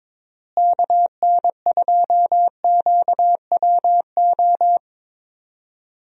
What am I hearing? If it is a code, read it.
Morse code, heard as a keyed tone, about 22 wpm, 700 Hz. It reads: KN2QWO
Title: USA Call Signs - 6 Characters - Repeated 2x - 22wpm